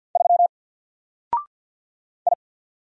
Morse code, heard as a keyed tone, about 48 wpm, 700 Hz. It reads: 3 I
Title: Single Letter-Number Alt 50wpm